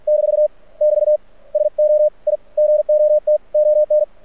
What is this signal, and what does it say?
Morse code, heard as a keyed tone, about 48 wpm, 600 Hz. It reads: - <BT> SO I GOT ON
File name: archived / CRSnet